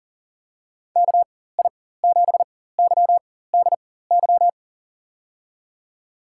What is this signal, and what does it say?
Morse code, heard as a keyed tone, about 39 wpm, spaced as for 18 wpm, 700 Hz. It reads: KI7YDY
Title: USA Call Signs - 6 Characters - 4x Character Spacing - 40wpm